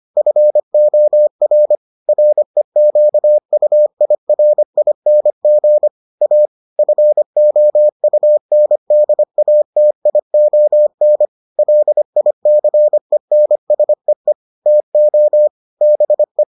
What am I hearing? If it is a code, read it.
Morse code, heard as a keyed tone, about 25 wpm, 600 Hz. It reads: FOR REQUIRING A FOUNDATION LICENSEE TO BE